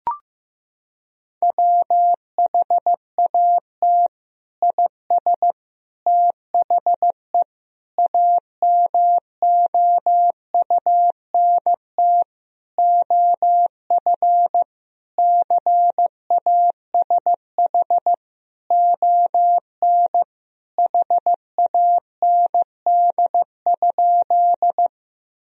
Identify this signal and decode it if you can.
Morse code, heard as a keyed tone, about 15 wpm, 700 Hz. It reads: WHAT IS THE AMOUNT OF CASH ON HAND?
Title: Sentences from Top 1000 Words - Review Part 4 - 15wpm